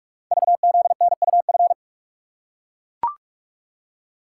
Morse code, heard as a keyed tone, about 44 wpm, 690 Hz. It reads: VZNUF
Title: Code Groups - 5 Letter:Numbers - 45wpm